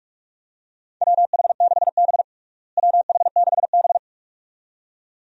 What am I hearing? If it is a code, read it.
Morse code, heard as a keyed tone, about 41 wpm, 700 Hz. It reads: WH6B WH6B
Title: USA Call Signs - 4 Characters - Repeated 2x - 45wpm